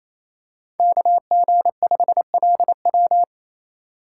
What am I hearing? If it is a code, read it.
Morse code, heard as a keyed tone, about 28 wpm, 700 Hz. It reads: KG5LW
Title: USA Call Signs - 5 Characters - 4x Character Spacing - 28wpm